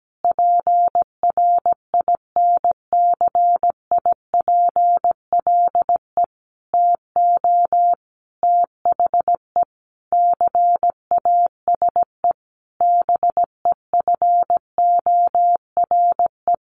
Morse code, heard as a keyed tone, about 17 wpm, 700 Hz. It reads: PRINCIPLE TO THE CASE BEFORE
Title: Sentences from Top 1400 Words 17wpm